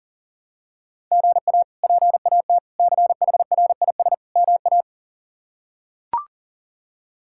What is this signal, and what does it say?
Morse code, heard as a keyed tone, about 40 wpm, 700 Hz. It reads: GA PAT CHRIS MA
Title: K1USN SST QSO Practice - 40wpm